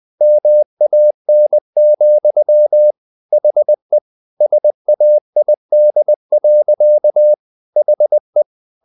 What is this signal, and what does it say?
Morse code, heard as a keyed tone, about 20 wpm, 600 Hz. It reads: MAN, HE SAID. HE
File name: the old man and the sea part 2 20wpm